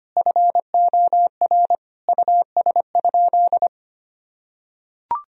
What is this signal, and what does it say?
Morse code, heard as a keyed tone, about 25 wpm, 700 Hz. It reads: FOR US?
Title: Sentences from Top 1000 Words - Review Part 2 - 25wpm